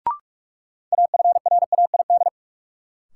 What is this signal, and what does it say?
Morse code, heard as a keyed tone, about 46 wpm, 700 Hz. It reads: AFRAID
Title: Sentences from Top 1500 Words - New Word Drill - 45wpm